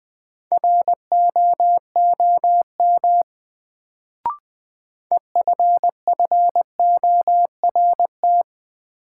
Morse code, heard as a keyed tone, about 20 wpm, 700 Hz. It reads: ROOM EFFORT
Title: Sets of 4 Words - New Word Drill - 20wpm